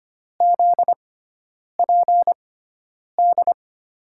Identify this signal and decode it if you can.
Morse code, heard as a keyed tone, about 25 wpm, spaced as for 9 wpm, 700 Hz. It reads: ZPD
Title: USA Call Signs - 6 Characters - 6x Character Spacing - 25wpm